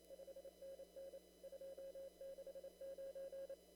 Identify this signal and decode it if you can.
Morse code, heard as a keyed tone, about 29 wpm, 540 Hz. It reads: 5NN 269
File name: CW / kt5j wpx cw-op-k5pi